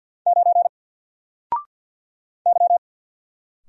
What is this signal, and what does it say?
Morse code, heard as a keyed tone, about 48 wpm, 700 Hz. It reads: 9 Y
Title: Single Character (Letter:Number:Punctuation:Prosign) 50wpm